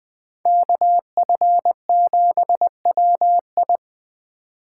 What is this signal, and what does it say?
Morse code, heard as a keyed tone, about 20 wpm, 700 Hz. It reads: KF7WI
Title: 2023 SOTA Chaser Call Signs - 20wpm